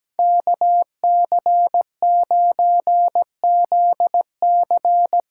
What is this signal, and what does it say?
Morse code, heard as a keyed tone, about 17 wpm, 700 Hz. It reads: KC9ZC
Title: USA Call Signs - 6 Characters - 8x Character Spacing - 17wpm